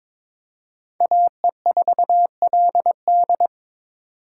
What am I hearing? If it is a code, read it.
Morse code, heard as a keyed tone, about 26 wpm, 700 Hz. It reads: AE4LD
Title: Speed Racing - USA Call Signs 22wpm